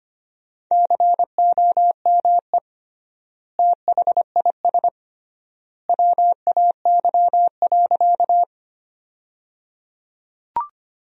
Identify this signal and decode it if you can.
Morse code, heard as a keyed tone, about 25 wpm, 700 Hz. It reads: COME THIS WAY.
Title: 3x Word-Spacing - Sentences from Top 100 Words - 25wpm